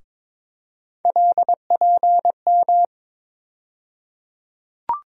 Morse code, heard as a keyed tone, about 22 wpm, 700 Hz. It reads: LPM
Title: USA Call Sign Postfixes - 3 Letters 22wpm